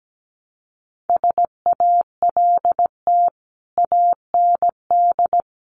Morse code, heard as a keyed tone, about 17 wpm, 700 Hz. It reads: SALT AND
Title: Sets of 5 Words Encore 17wpm